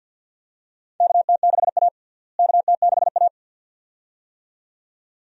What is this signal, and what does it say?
Morse code, heard as a keyed tone, about 47 wpm, 700 Hz. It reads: KT6A KT6A
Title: USA Call Signs - 4 Characters - Repeated 2x - 50wpm